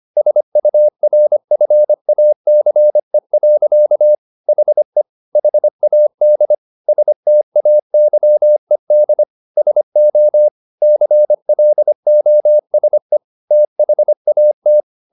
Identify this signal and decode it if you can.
Morse code, heard as a keyed tone, about 25 wpm, 600 Hz. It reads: SURFACE. HE HAD STAYED SO CLOSE THAT
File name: the old man and the sea part 1 25wpm